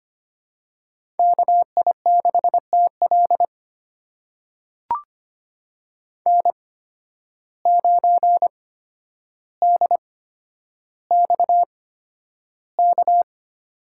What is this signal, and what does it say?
Morse code, heard as a keyed tone, about 25 wpm, spaced as for 11 wpm, 700 Hz. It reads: KI6TL N9DXK
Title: USA Call Signs - 5 Characters - 8x Character Spacing - 25wpm